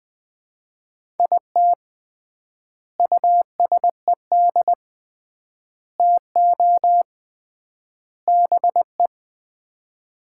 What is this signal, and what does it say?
Morse code, heard as a keyed tone, about 20 wpm, 700 Hz. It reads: IT USED TO BE
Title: 3x Word-Spacing - Sets of 4 Words - 20wpm